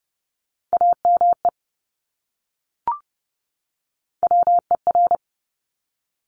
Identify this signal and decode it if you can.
Morse code, heard as a keyed tone, about 32 wpm, 690 Hz. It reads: AME WER
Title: Top Three Letter Combinations in Words 30wpm